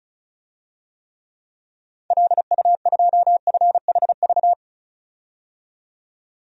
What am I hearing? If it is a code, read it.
Morse code, heard as a keyed tone, about 36 wpm, 700 Hz. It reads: LU2FHV
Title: DXCC-100 Call Signs - 6 Characters - 35wpm